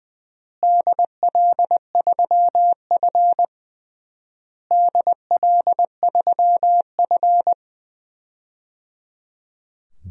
Morse code, heard as a keyed tone, about 19 wpm, 700 Hz. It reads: DL3F DL3F
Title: DXCC-100 Call Signs - 4 Characters - Repeated 3x - 20wpm